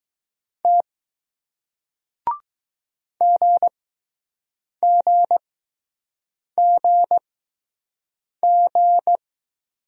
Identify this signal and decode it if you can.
Morse code, heard as a keyed tone, about 20 wpm, spaced as for 6 wpm, 700 Hz. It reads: T GGGG